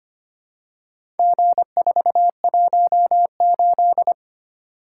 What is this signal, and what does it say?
Morse code, heard as a keyed tone, about 25 wpm, 700 Hz. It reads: G418
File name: practice / 25wpm / Four Letter-Number Combinations - 25wpm